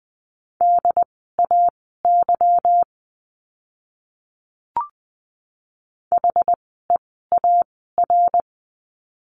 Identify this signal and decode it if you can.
Morse code, heard as a keyed tone, about 20 wpm, spaced as for 14 wpm, 700 Hz. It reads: DAY HEAR
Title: 206 Lesson 33f - Letter j - Words 20wpm